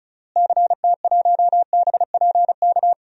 Morse code, heard as a keyed tone, about 36 wpm, 700 Hz. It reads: CT1BPK
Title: DXCC-100 Call Signs - 6 Characters - Repeated 3x - 35wpm